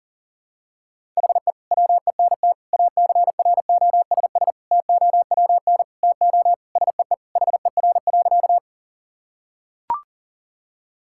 Morse code, heard as a keyed tone, about 40 wpm, 700 Hz. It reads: HE WENT ACROSS TOWN TO SEE HER.
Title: Sentences from Top 1000 Words - Review Part 5 - 40wpm